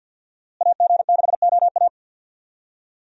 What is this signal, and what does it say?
Morse code, heard as a keyed tone, about 47 wpm, 700 Hz. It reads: AG6OA
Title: USA Call Signs - 5 Characters - Repeated 2x - 50wpm